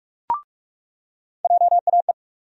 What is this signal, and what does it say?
Morse code, heard as a keyed tone, about 47 wpm, 700 Hz. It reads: JAE
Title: USA Call Sign Postfixes - 3 Letters 45wpm